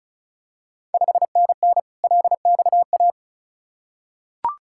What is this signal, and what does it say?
Morse code, heard as a keyed tone, about 36 wpm, 700 Hz. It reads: 5NN LXA
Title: Ontario QSO Party Exchange 35wpm